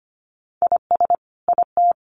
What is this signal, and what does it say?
Morse code, heard as a keyed tone, about 25 wpm, 700 Hz. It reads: IS IT
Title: Sentences from Top 600 Words 25wpm